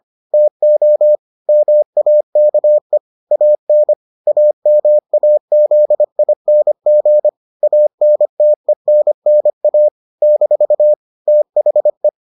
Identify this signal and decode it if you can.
Morse code, heard as a keyed tone, about 25 wpm, 600 Hz. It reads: TO MAKE AN AMAZING ANTENNA - THE